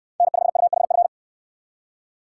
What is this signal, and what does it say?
Morse code, heard as a keyed tone, about 64 wpm, 700 Hz. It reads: N5UHF